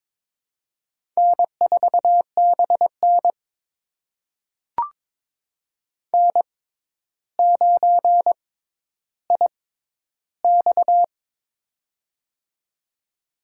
Morse code, heard as a keyed tone, about 22 wpm, spaced as for 11 wpm, 700 Hz. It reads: N4BN N9IX
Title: USA Call Signs - 4 Characters - 6x Character Spacing - 22wpm